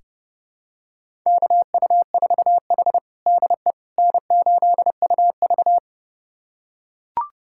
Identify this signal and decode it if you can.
Morse code, heard as a keyed tone, about 32 wpm, 700 Hz. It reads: KU4H DE N8UV
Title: USA Callsign DE Callsign - 4 Characters - 30wpm